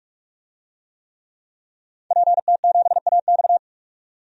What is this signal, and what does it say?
Morse code, heard as a keyed tone, about 46 wpm, 700 Hz. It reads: PT7AX
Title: DXCC-100 Call Signs - 5 Characters - 45wpm